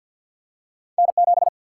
Morse code, heard as a keyed tone, about 53 wpm, 700 Hz. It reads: N7
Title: Single-Letter Single-Number 50wpm